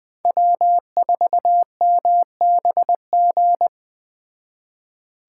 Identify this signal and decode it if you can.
Morse code, heard as a keyed tone, about 20 wpm, 700 Hz. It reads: W4MBG